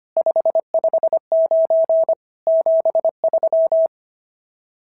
Morse code, heard as a keyed tone, about 22 wpm, 650 Hz. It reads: 559 73
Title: SOTA QSO Practice 25wpm